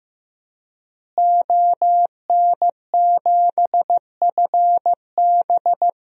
Morse code, heard as a keyed tone, about 15 wpm, 700 Hz. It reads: ON7FB
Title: DXCC-100 Call Signs - 5 Characters - Repeated 3x - 15wpm